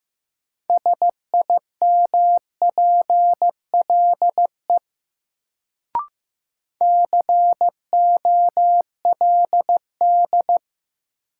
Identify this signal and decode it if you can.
Morse code, heard as a keyed tone, about 15 wpm, 700 Hz. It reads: SIMPLE COLD
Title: Sentences from Top 400 Words - New Word Drill - 15wpm